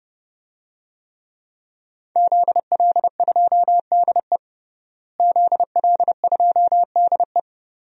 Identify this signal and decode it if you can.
Morse code, heard as a keyed tone, about 30 wpm, 700 Hz. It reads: ZL2DE ZL2DE
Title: DXCC-100 Call Signs - 5 Characters - Repeated 3x - 30wpm